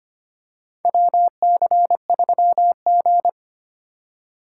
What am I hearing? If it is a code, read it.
Morse code, heard as a keyed tone, about 25 wpm, 700 Hz. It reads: WC3G
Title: USA Call Signs - 4 Characters - 4x Character Spacing - 25wpm